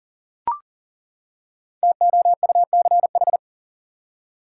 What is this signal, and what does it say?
Morse code, heard as a keyed tone, about 39 wpm, 700 Hz. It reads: TOUCH
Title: Five Letter Words - 40wpm